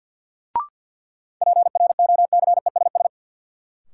Morse code, heard as a keyed tone, about 53 wpm, 700 Hz. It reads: PROCESS
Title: Sentences from Top 1000 Words - Review Part 3 - New Word Drill - 50wpm